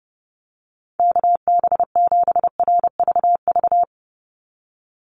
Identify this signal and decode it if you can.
Morse code, heard as a keyed tone, about 30 wpm, 700 Hz. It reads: KB7RVV
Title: USA Call Signs 30wpm